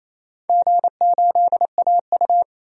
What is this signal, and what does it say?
Morse code, heard as a keyed tone, about 28 wpm, 700 Hz. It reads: G8AU
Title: DXCC-100 Call Signs - 4 Characters - 28wpm